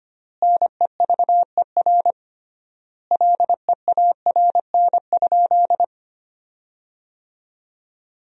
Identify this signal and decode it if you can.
Morse code, heard as a keyed tone, about 25 wpm, 700 Hz. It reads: NEVER LEARN?
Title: 3x Word-Spacing - Sentences from Top 300 Words - 25wpm